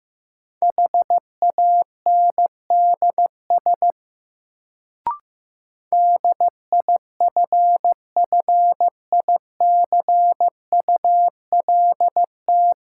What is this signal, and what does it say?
Morse code, heard as a keyed tone, about 15 wpm, 700 Hz. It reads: HANDS DIFFICULT
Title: Sets of 4 Words - New Word Drill - 15wpm